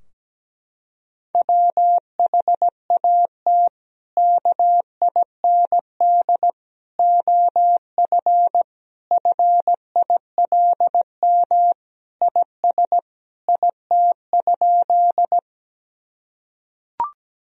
Morse code, heard as a keyed tone, about 17 wpm, 700 Hz. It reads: WHAT KIND OF FILM IS IT?
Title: Sentences from Top 1000 Words - Review Part 3 - 17wpm